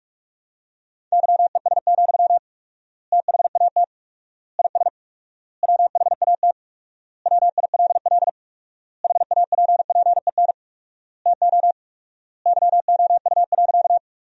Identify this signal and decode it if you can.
Morse code, heard as a keyed tone, about 46 wpm, 700 Hz. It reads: YES, THAT IS WHAT WILL HAPPEN TO YOU.
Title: 4x Word-Spacing - Sentences from Top 500 Words - 45wpm